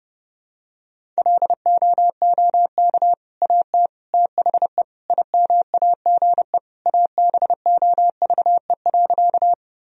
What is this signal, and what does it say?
Morse code, heard as a keyed tone, about 30 wpm, 700 Hz. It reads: LOOK AT THE IMAGE ABOVE.
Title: Sentences from Top 1000 Words - Review Part 4 - 30wpm